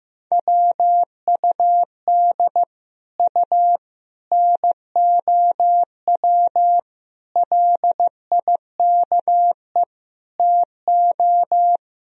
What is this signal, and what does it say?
Morse code, heard as a keyed tone, about 15 wpm, 700 Hz. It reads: WUD U NOW LIKE TO
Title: Sentences from Top 200 Words and Abbreviations 15wpm